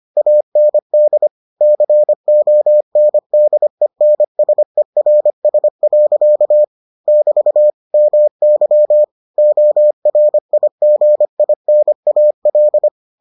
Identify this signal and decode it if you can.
Morse code, heard as a keyed tone, about 25 wpm, 600 Hz. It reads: AND CONDENSERS. <BT> MY ORIGINAL